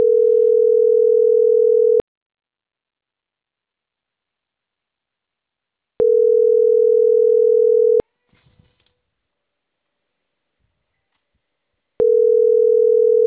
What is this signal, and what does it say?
Ringback tone (North America), 440 + 480 Hz.